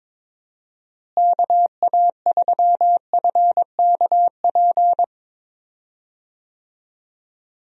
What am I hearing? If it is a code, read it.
Morse code, heard as a keyed tone, about 22 wpm, 700 Hz. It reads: KA3FKP
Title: USA Call Signs - 6 Characters - Repeated 3x - 22wpm